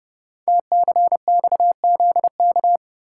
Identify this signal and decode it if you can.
Morse code, heard as a keyed tone, about 30 wpm, 700 Hz. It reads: TCXZK
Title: Code Groups - 5 Letters - 30wpm